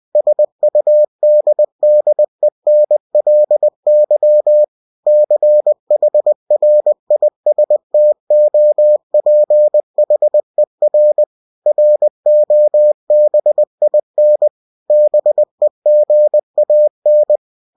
Morse code, heard as a keyed tone, about 20 wpm, 600 Hz. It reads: SUDDENLY CHRISTOPHER ROBIN BEGAN
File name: cw-books / winnie-the-pooh / winnie-20wpm-page-0043